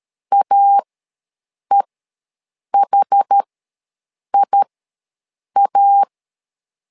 Morse code, heard as a keyed tone, about 13 wpm, spaced as for 7 wpm, 800 Hz. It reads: AEHIA
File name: LEZIONI / 2 UHDN / 2 UH ws 12wpm 1730